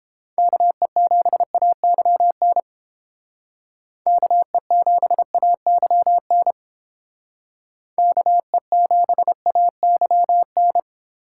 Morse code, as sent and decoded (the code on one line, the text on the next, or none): -.- . --... .- -.-- -. / -.- . --... .- -.-- -. / -.- . --... .- -.-- -.
KE7AYN KE7AYN KE7AYN